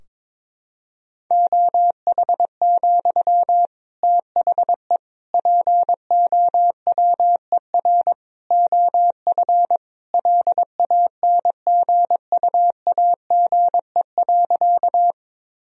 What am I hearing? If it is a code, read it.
Morse code, heard as a keyed tone, about 22 wpm, 700 Hz. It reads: OH, THE POWER OF LANGUAGE.
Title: Sentences from Top 1000 Words - Review Part 3 - 22wpm